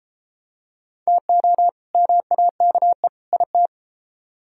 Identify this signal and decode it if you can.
Morse code, heard as a keyed tone, about 32 wpm, 700 Hz. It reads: TO MAKE IT